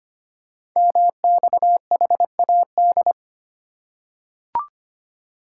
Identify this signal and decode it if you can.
Morse code, heard as a keyed tone, about 25 wpm, 700 Hz. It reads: MXHAD
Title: Code Groups - 5 Letters - 25wpm